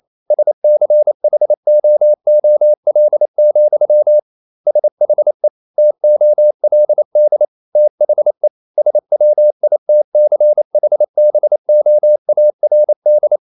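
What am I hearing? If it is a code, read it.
Morse code, heard as a keyed tone, about 28 wpm, 600 Hz. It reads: SCHOOL, SHE TOLD THE SWITCHBOARD
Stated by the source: Morse code practice at 28 wpm, text of Ken Follett's novel Never in English, from CW med Gnister.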